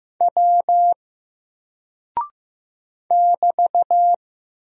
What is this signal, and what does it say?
Morse code, heard as a keyed tone, about 15 wpm, 700 Hz. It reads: W <BT>
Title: Single Character (Letter:Number:Punctuation:Prosign) 15wpm